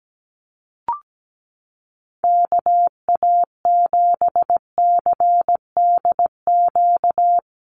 Morse code, heard as a keyed tone, about 17 wpm, 700 Hz. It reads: KA7CDQ
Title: USA Call Signs Repeated 2x 17wpm